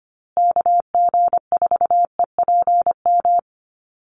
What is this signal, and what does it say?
Morse code, heard as a keyed tone, about 25 wpm, 700 Hz. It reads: KG4EPM
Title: USA Call Signs Repeated 2x 25wpm